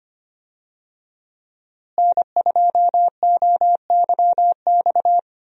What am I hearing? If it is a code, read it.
Morse code, heard as a keyed tone, about 25 wpm, 700 Hz. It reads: N2OYX
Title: USA Call Signs - 5 Characters - Repeated 2x - 25wpm